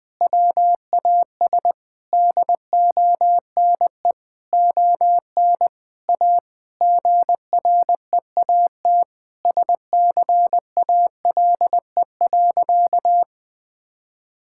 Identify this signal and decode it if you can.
Morse code, heard as a keyed tone, about 20 wpm, 700 Hz. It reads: WAS DONE ON A GREAT SCALE.
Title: Sentences from Top 1000 Words - Review Part 5 - 20wpm